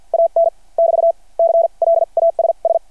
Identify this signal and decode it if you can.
Morse code, heard as a keyed tone, about 50 wpm, spaced as for 40 wpm, 670 Hz. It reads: AN <BT> CLASS
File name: archived / CRSnet